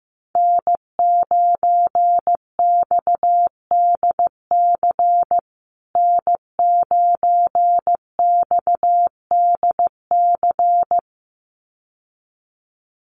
Morse code, heard as a keyed tone, about 15 wpm, 700 Hz. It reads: N9XDC N9XDC
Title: USA Call Signs Repeated 2x 15wpm